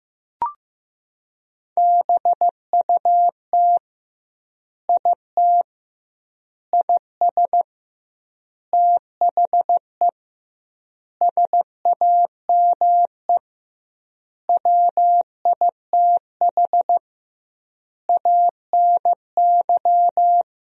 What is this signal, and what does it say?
Morse code, heard as a keyed tone, about 15 wpm, 700 Hz. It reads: BUT IT IS THE SAME WITH ANY
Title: 2x Word-Spacing - Sentences from Top 300 Words - 15wpm